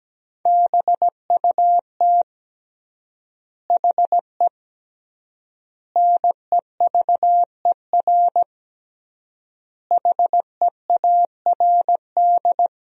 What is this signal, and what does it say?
Morse code, heard as a keyed tone, about 17 wpm, 700 Hz. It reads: BUT HE NEVER HEARD
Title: 3x Word-Spacing - Sentences from Top 400 Words - 17wpm